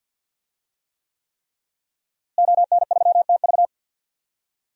Morse code, heard as a keyed tone, about 49 wpm, 700 Hz. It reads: ON3TV
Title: DXCC-100 Call Signs - 5 Characters - Repeated 3x - 50wpm